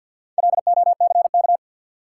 Morse code, heard as a keyed tone, about 53 wpm, 700 Hz. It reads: LOCK
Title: Sentences from Top 1300 Words - New Word Drill - 50wpm